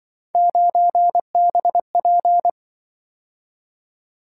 Morse code, heard as a keyed tone, about 24 wpm, 700 Hz. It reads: 9BP
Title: Speed-Racing - Three Letter-Number Combinations - 20wpm